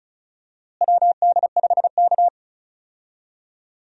Morse code, heard as a keyed tone, about 34 wpm, 700 Hz. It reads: WD5K
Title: USA Call Signs - 4 Characters - Repeated 2x - 35wpm